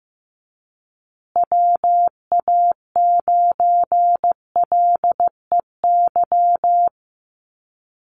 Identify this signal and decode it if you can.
Morse code, heard as a keyed tone, about 15 wpm, 700 Hz. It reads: WA9LEY
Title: Call Signs Repeated 2x - 15wpm